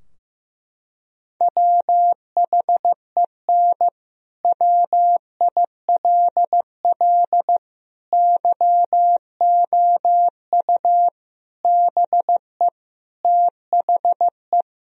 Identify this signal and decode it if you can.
Morse code, heard as a keyed tone, about 15 wpm, 700 Hz. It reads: WHEN WILL YOU BE THE
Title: Sentences from Top 1000 Words - Review Part 4 - 15wpm